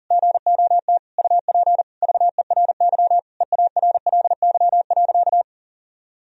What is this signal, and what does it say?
Morse code, heard as a keyed tone, about 40 wpm, 700 Hz. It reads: GOT UP VERY EARLY.
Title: Sentences from Top 1000 Words - Review Part 2 - 40wpm